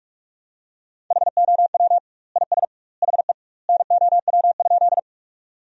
Morse code, heard as a keyed tone, about 46 wpm, 700 Hz. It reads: HOW IS HE NOW?
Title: 2x Word-Spacing - Sentences from Top 100 Words - 45wpm